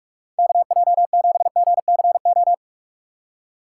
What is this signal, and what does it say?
Morse code, heard as a keyed tone, about 46 wpm, 700 Hz. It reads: KJ7GCO
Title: USA Call Signs - 6x Character Spacing - 45wpm